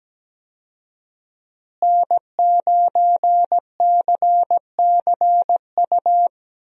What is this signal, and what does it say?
Morse code, heard as a keyed tone, about 17 wpm, 700 Hz. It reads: N9CCU
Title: USA Call Signs - 5 Characters - Repeated 3x - 17wpm